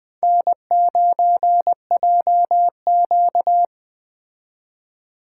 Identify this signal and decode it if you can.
Morse code, heard as a keyed tone, about 20 wpm, 700 Hz. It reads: N9JQ
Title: Speed Racing - USA 1x2 Call Signs 17wpm